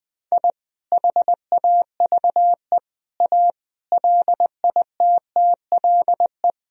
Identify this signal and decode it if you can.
Morse code, heard as a keyed tone, about 20 wpm, 700 Hz. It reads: I HAVE A LITTLE
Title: Sentences from Top 1000 Words - Review Part 1 - 20wpm